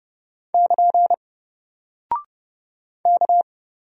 Morse code, heard as a keyed tone, about 30 wpm, 700 Hz. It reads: <KN> K
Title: Single Character (Letter:Number:Punctuation:Prosign) 30wpm